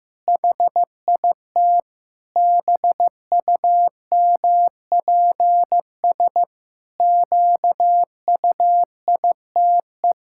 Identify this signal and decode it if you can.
Morse code, heard as a keyed tone, about 15 wpm, 700 Hz. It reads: HIT BUMPS QUITE
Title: Pangram Challenge v2 - 15wpm